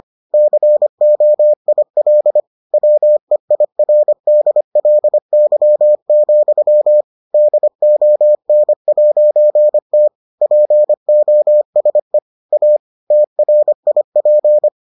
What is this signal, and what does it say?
Morse code, heard as a keyed tone, about 25 wpm, 600 Hz. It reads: COIL WEIRDLY, DON'T POSE A TRIP